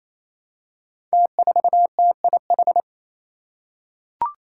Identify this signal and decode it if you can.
Morse code, heard as a keyed tone, about 28 wpm, 700 Hz. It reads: T4TIH
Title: Five Challenging Letter-Number Combinations 28wpm